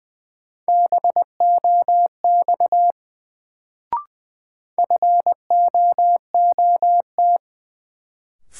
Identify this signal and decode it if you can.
Morse code, heard as a keyed tone, about 20 wpm, 700 Hz. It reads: BOX FOOT
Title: Sentences from Top 500 Words - New Word Drill - 20wpm